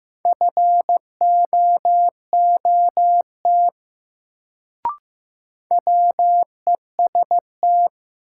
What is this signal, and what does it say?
Morse code, heard as a keyed tone, about 15 wpm, 700 Hz. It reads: FOOT WEST
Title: Sets of 5 Words - New Word Drill - 15wpm